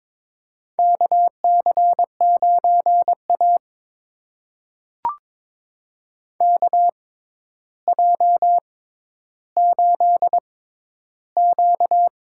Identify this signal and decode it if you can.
Morse code, heard as a keyed tone, about 22 wpm, spaced as for 11 wpm, 700 Hz. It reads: KC9A KJ8Q
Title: USA Call Signs - 4 Characters - 6x Character Spacing - 22wpm